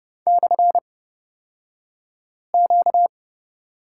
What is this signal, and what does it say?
Morse code, heard as a keyed tone, about 30 wpm, spaced as for 5 wpm, 700 Hz. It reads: /Q
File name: Lesson 38d - Letter q - Two Characters 30wpm